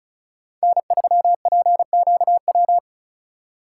Morse code, heard as a keyed tone, about 41 wpm, 700 Hz. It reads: N3PQW
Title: Speed Racing - USA 1x3 Call Signs 35wpm